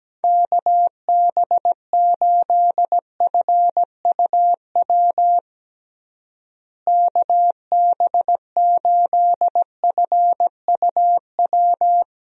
Morse code, heard as a keyed tone, about 17 wpm, 700 Hz. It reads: KB8FUW KB8FUW
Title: USA Call Signs - 6 Characters - Repeated 3x - 17wpm